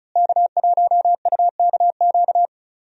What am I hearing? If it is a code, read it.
Morse code, heard as a keyed tone, about 34 wpm, 700 Hz. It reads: K1UKQ
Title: USA Call Signs - 5 Characters - Repeated 2x - 35wpm